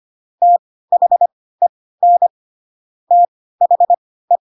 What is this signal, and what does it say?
Morse code, heard as a keyed tone, about 25 wpm, spaced as for 16 wpm, 700 Hz. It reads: THEN THE
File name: Pinocchio / Pinocchio ch3 10 25 ik1hge